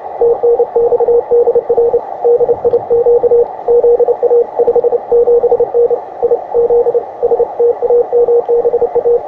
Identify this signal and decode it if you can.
Morse code, heard as a keyed tone, about 29 wpm, 470 Hz, with radio noise. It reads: TNX DR DIQ ZA 57N I ZSTAMBA
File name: attachments / ua6ypt 15m